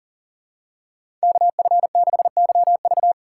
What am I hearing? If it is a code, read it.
Morse code, heard as a keyed tone, about 40 wpm, 700 Hz. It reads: KF6YV
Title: USA Call Signs - 2x Character Spacing - 40wpm